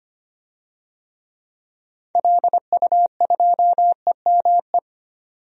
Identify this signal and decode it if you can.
Morse code, heard as a keyed tone, about 25 wpm, 690 Hz. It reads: LU2EME
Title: DXCC-100 Call Signs - 6 Characters - 25wpm